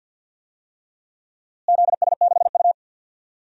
Morse code, heard as a keyed tone, about 50 wpm, 700 Hz. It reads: ZS6U
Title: DXCC-100 Call Signs - 4 Characters - Repeated 3x - 50wpm